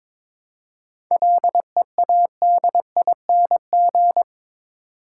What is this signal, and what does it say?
Morse code, heard as a keyed tone, about 22 wpm, 700 Hz. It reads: LEADING
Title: Seven Letter Words - 22wpm